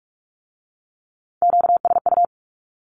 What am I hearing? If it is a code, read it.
Morse code, heard as a keyed tone, about 46 wpm, 700 Hz. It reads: QSU
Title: Q-Codes with Spoken Meaning - 45wpm